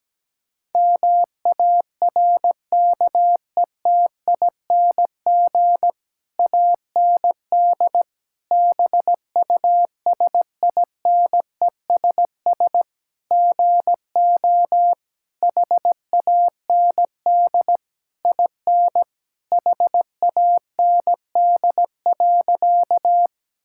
Morse code, heard as a keyed tone, about 17 wpm, 700 Hz. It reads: MARKETING AND BUSINESS GO HAND IN HAND.
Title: Sentences from Top 1000 Words - Review Part 5 - 17wpm